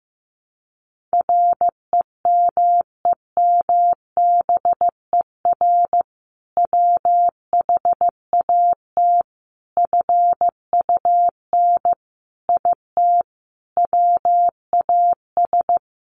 Morse code, heard as a keyed tone, about 15 wpm, 700 Hz. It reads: REMEMBER WHAT FUN IT WAS
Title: Sentences from Top 700 Words 15wpm